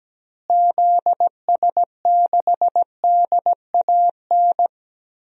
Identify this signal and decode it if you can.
Morse code, heard as a keyed tone, about 17 wpm, 700 Hz. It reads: ZS6DAN
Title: DXCC-100 Call Signs - 6 Characters - Repeated 3x - 17wpm